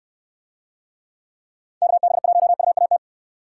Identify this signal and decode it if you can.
Morse code, heard as a keyed tone, about 66 wpm, 700 Hz. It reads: KB1UAT